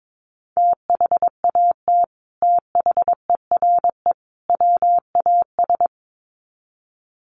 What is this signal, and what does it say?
Morse code, heard as a keyed tone, about 22 wpm, 700 Hz. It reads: THAT THERE WAS
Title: Sets of 3 Words 22wpm